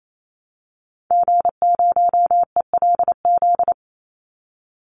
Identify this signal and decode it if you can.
Morse code, heard as a keyed tone, about 28 wpm, 700 Hz. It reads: G0ELZ
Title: Call Signs Repeated 3x - 28wpm